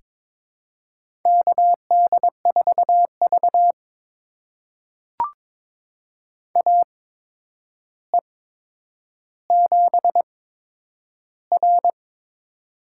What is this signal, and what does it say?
Morse code, heard as a keyed tone, about 22 wpm, spaced as for 9 wpm, 700 Hz. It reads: KD4V AE7R
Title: USA Call Signs - 4 Characters - 8x Character Spacing - 22wpm